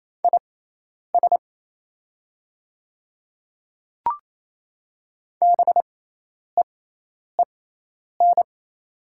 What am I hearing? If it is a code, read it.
Morse code, heard as a keyed tone, about 27 wpm, spaced as for 10 wpm, 700 Hz. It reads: IS BEEN
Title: 6x Character Spacing - Top 300 Words - 28wpm